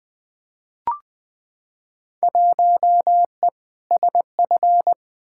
Morse code, heard as a keyed tone, about 20 wpm, 700 Hz. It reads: 1E SF
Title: Field Day Exchanges - 20wpm